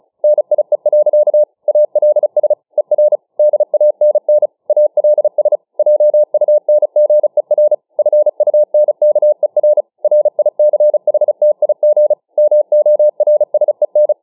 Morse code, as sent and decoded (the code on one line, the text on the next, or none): -. .. .-.-.- / .- .-.. ... / . .-. / -.. .- -. -. / .- .-.. ... / .--- ..- -. --. .-. / ..-. ..- -. -.- . .-. / .-. .. -.-. .... - .. --. / -- --- .-. ... -.
NIE. ALS ER DANN ALS JUNGER FUNKER RICHTIG MORSEN